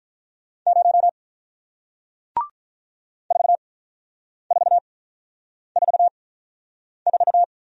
Morse code, heard as a keyed tone, about 48 wpm, spaced as for 9 wpm, 700 Hz. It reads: 0 4444